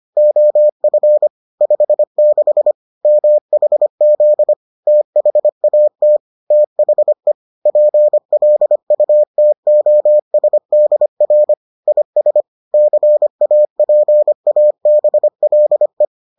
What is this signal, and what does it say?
Morse code, heard as a keyed tone, about 25 wpm, 600 Hz. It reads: OF 56 MHZ THAT THE PLUTOSDR IS CAPABLE